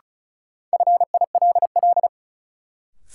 Morse code, heard as a keyed tone, about 36 wpm, 700 Hz. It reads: FILL
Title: Sentences from Top 1500 Words - New Word Drill Alt - 35wpm